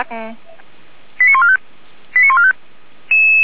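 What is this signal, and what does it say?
Five-tone selective call (ZVEI), heard as quick runs of five tones, tones of about 70 ms, with speech. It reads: 89145 89145
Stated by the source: Received on an SDR.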